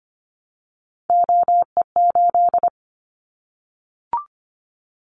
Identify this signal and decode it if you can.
Morse code, heard as a keyed tone, about 25 wpm, 700 Hz. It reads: OE8
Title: Callsign Prefixes for Top 25 DXCC - 25wpm